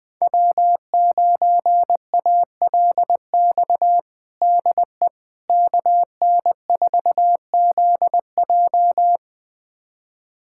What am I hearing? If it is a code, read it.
Morse code, heard as a keyed tone, about 20 wpm, 700 Hz. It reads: W9ALX DE KN4ZJ
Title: USA Callsign DE Callsign - 5 Characters - 20wpm